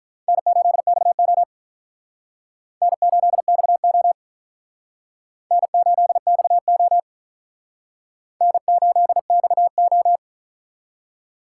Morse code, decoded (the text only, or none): N8XO N8XO N8XO N8XO